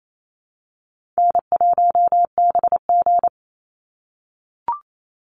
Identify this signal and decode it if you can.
Morse code, heard as a keyed tone, about 28 wpm, 700 Hz. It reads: N1BG
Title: Easy Call Signs - 28wpm